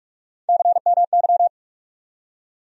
Morse code, heard as a keyed tone, about 46 wpm, 700 Hz. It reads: CMY
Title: USA Call Sign Postfixes - 3 Letters 45wpm